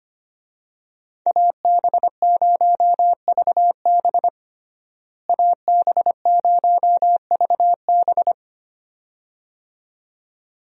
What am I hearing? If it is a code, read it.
Morse code, heard as a keyed tone, about 25 wpm, 700 Hz. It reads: AB0VB AB0VB
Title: USA Call Signs - 5 Characters - Repeated 2x - 25wpm